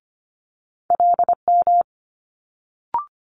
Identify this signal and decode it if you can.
Morse code, heard as a keyed tone, about 25 wpm, 700 Hz. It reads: LM
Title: Multi-letter Phonemes Quick Fire 25wpm